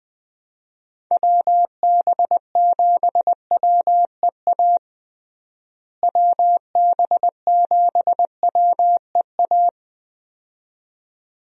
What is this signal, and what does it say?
Morse code, heard as a keyed tone, about 20 wpm, 700 Hz. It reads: WB7WEA WB7WEA
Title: USA Call Signs - 6 Characters - Repeated 2x - 20wpm